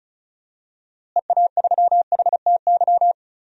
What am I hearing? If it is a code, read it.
Morse code, heard as a keyed tone, about 36 wpm, 700 Hz. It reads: EA3HTY
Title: DXCC-100 Call Signs - 6 Characters - Repeated 2x - 35wpm